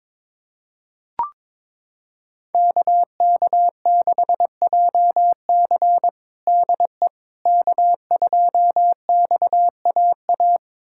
Morse code, heard as a keyed tone, about 22 wpm, 700 Hz. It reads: KK6JC DE K2XAA
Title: USA Callsign DE Callsign - 5 Characters - 22wpm